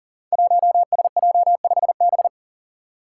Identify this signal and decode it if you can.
Morse code, heard as a keyed tone, about 40 wpm, 700 Hz. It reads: 1SJ5B
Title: Five Challenging Letter-Number Combinations 40wpm